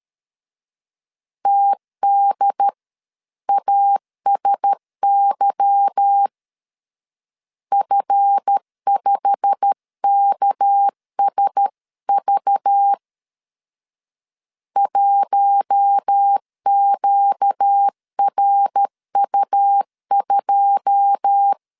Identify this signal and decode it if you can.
Morse code, heard as a keyed tone, about 13 wpm, 800 Hz. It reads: TDASY F5KSV 1QRU2
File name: CW / DET / 1 9 1500C S B 12wpm 3819